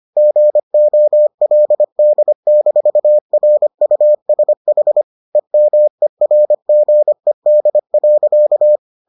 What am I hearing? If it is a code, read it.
Morse code, heard as a keyed tone, about 25 wpm, 600 Hz. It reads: GOLD-RUSH EMERGED.